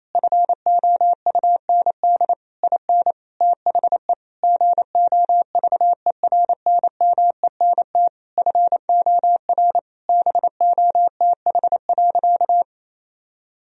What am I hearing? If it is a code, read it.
Morse code, heard as a keyed tone, about 28 wpm, 700 Hz. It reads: FOUND IN THE GOVERNMENT FOR BOTH.
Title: Sentences from Top 1000 Words - Review Part 3 - 28wpm